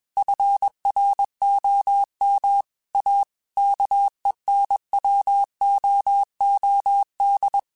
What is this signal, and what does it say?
Morse code, heard as a keyed tone, about 21 wpm, 800 Hz. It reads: FROM A KENWOOD
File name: QSOs / QSOs-mp3s / kc7haa@20